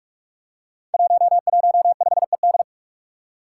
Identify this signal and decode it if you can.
Morse code, heard as a keyed tone, about 41 wpm, 700 Hz. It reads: 115ED
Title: Five Challenging Letter-Number Combinations 45wpm